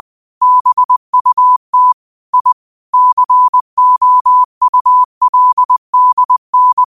Morse code, heard as a keyed tone, about 20 wpm, 1000 Hz. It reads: BUT I COULDN
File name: qotdfiles / 20wpm